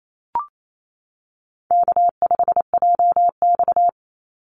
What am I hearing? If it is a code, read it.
Morse code, heard as a keyed tone, about 28 wpm, 700 Hz. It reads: K5JX
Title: Call Signs - 28wpm